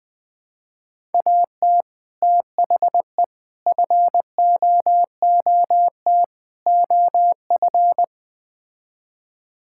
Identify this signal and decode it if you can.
Morse code, heard as a keyed tone, about 20 wpm, 700 Hz. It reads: AT THE FOOT OF
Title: Build A Phrase - 5-Word Phrases - 20wpm